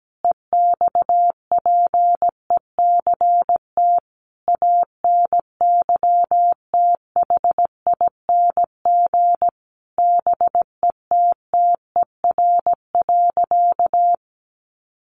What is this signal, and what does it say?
Morse code, heard as a keyed tone, about 17 wpm, 700 Hz. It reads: EXPECT ANYTHING BETTER.
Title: Sentences from Top 1100 Words 17wpm